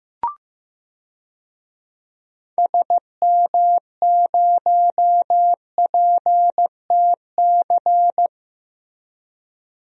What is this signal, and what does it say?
Morse code, heard as a keyed tone, about 15 wpm, 700 Hz. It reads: SM0PTC
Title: DXCC-100 Call Signs - 6 Characters - Repeated 3x - 15wpm